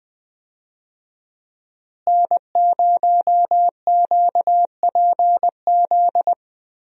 Morse code, heard as a keyed tone, about 20 wpm, 700 Hz. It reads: N0QPZ
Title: USA Call Signs - 5 Characters - Repeated 3x - 20wpm